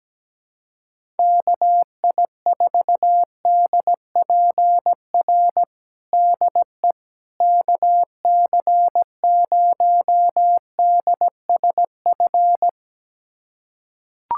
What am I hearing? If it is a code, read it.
Morse code, heard as a keyed tone, about 17 wpm, 700 Hz. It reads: KI4DPR DE KC0DSF
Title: USA Callsign DE Callsign 17wpm